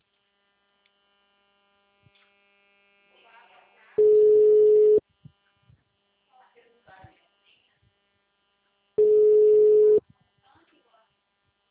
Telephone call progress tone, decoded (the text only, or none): ringback tone